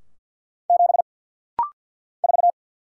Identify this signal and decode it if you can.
Morse code, heard as a keyed tone, about 50 wpm, 700 Hz. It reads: 7 4
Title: Single Letter-Number Alt - Rapid-Fire 50wpm